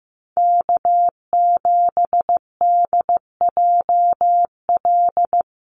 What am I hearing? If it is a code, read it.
Morse code, heard as a keyed tone, about 15 wpm, 700 Hz. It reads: K7DJL
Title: NA SOTA Chaser Callsigns 15wpm